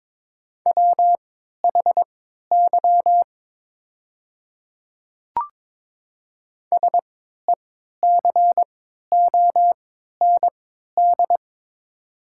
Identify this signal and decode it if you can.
Morse code, heard as a keyed tone, about 22 wpm, spaced as for 12 wpm, 700 Hz. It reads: WHY SECOND
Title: 3x Character Spacing - Top 500 Words - 22wpm